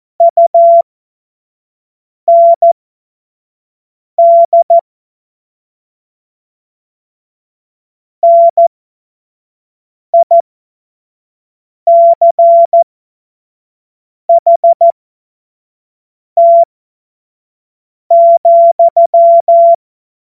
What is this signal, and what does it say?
Morse code, heard as a keyed tone, about 14 wpm, spaced as for 5 wpm, 680 Hz. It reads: UND NICHT,
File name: Morsekurs / Witze / Witz 02. Februar 2021, 25 F70